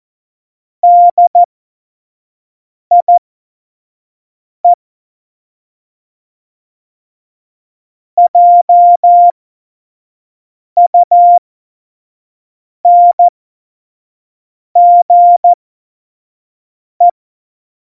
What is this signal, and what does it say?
Morse code, heard as a keyed tone, about 14 wpm, spaced as for 5 wpm, 700 Hz. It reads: DIE JUNGE
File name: Morsekurs / Witze / Witz 15. September 2020, 25 F70